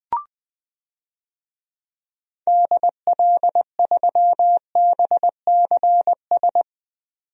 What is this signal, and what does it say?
Morse code, heard as a keyed tone, about 21 wpm, 700 Hz. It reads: DL3BCS
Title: DXCC-100 Call Signs - 6 Characters - 20wpm